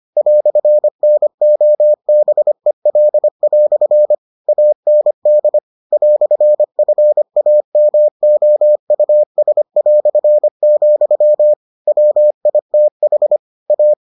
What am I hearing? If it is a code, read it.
Morse code, heard as a keyed tone, about 25 wpm, 600 Hz. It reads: "NOBEL" AND "FAMOUS", WITH A